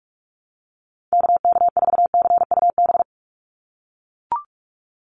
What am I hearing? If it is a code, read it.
Morse code, heard as a keyed tone, about 44 wpm, 700 Hz. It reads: KK4CUB
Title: USA Call Signs 45wpm